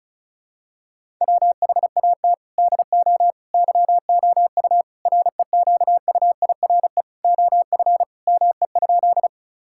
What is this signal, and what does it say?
Morse code, heard as a keyed tone, about 34 wpm, 700 Hz. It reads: WHAT DO YOU REQUIRE OF ME?
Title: Sentences from Top 1000 Words - Review Part 6 - 35wpm